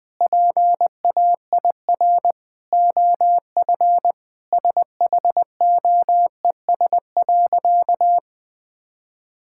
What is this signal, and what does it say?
Morse code, heard as a keyed tone, about 20 wpm, 700 Hz. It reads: PAIR OF SHOES.
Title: Sentences with 3-Word Phrases - 20wpm